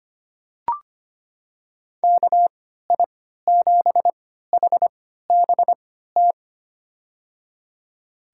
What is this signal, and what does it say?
Morse code, heard as a keyed tone, about 25 wpm, spaced as for 14 wpm, 700 Hz. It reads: KI7HBT
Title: USA Call Signs - 6 Characters - 3x Character Spacing - 25wpm